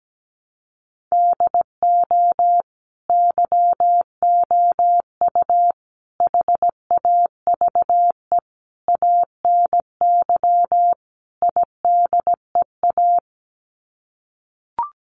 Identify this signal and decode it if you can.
Morse code, heard as a keyed tone, about 17 wpm, 700 Hz. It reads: DO YOU HAVE ANY IDEA
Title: Sets of 5 Words Encore 17wpm